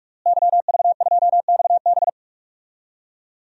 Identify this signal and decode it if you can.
Morse code, heard as a keyed tone, about 42 wpm, 700 Hz. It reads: YV2XB
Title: DXCC-100 Call Signs - 5 Characters - Repeated 2x - 45wpm